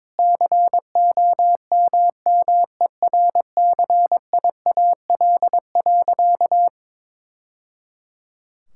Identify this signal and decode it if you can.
Morse code, heard as a keyed tone, about 22 wpm, 700 Hz. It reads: COMMERCIAL.
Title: Sentences from Top 1000 Words - Review Part 5 - 22wpm